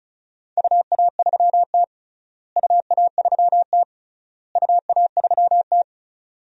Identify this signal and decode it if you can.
Morse code, heard as a keyed tone, about 34 wpm, 700 Hz. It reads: UA3T UA3T UA3T
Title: DXCC-100 Call Signs - 4 Characters - Repeated 3x - 35wpm